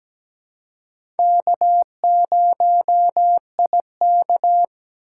Morse code, heard as a keyed tone, about 17 wpm, 700 Hz. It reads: K0IK
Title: USA Call Signs - 3x Character Spacing - 17wpm